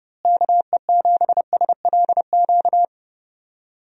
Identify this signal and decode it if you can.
Morse code, heard as a keyed tone, about 31 wpm, 700 Hz. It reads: KE7SLQ
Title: USA Call Signs - 6 Characters - Repeated 3x - 30wpm